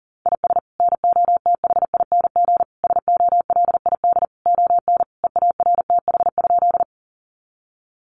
Morse code, heard as a keyed tone, about 40 wpm, 700 Hz. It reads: IS NOTHING SOLID ON EARTH?
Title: Sentences from Top 1500 Words 40wpm